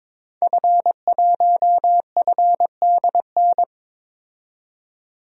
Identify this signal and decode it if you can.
Morse code, heard as a keyed tone, about 22 wpm, 700 Hz. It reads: F1FDN
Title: DXCC-100 Call Signs - 5 Characters - 22wpm